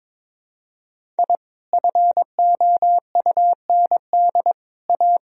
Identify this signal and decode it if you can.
Morse code, heard as a keyed tone, about 22 wpm, 700 Hz. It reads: I FOUND A
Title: Sentences from Top 1000 Words - Review Part 1 - 22wpm